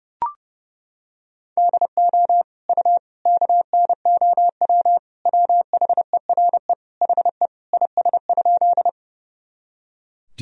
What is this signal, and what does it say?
Morse code, heard as a keyed tone, about 32 wpm, 700 Hz. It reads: DO U KNOW WHERE HE IS?
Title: Sentences from Top 300 Words and Abbreviations 30wpm